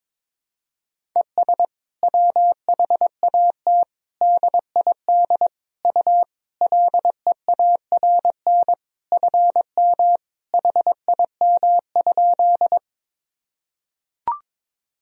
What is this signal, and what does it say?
Morse code, heard as a keyed tone, about 22 wpm, 700 Hz. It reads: ES WHAT DID U LEARN FM HIM?
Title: Sentences from Top 300 Words and Abbreviations 22wpm